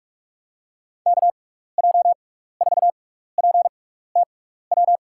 Morse code, heard as a keyed tone, about 47 wpm, spaced as for 16 wpm, 700 Hz. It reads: KJ4PTW
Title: USA Call Signs - 6x Character Spacing - 45wpm